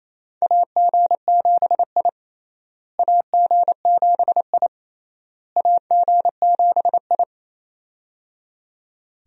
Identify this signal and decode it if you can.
Morse code, heard as a keyed tone, about 28 wpm, 700 Hz. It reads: AG7I AG7I AG7I
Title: USA Call Signs - 4 Characters - Repeated 3x - 28wpm